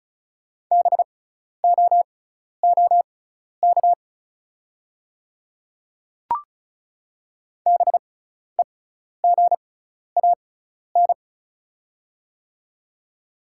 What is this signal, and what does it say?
Morse code, heard as a keyed tone, about 36 wpm, spaced as for 12 wpm, 690 Hz. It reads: BOOK BEGAN
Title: 6x Character Spacing - Top 300 Words - 35wpm